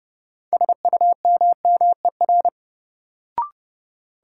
Morse code, heard as a keyed tone, about 30 wpm, 700 Hz. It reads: SUMMER
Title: Sentences from Top 1000 Words - Review Part 6 - New Word Drill - 30wpm